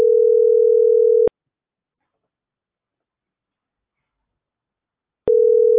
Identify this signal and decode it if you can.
Ringback tone (North America), 440 + 480 Hz.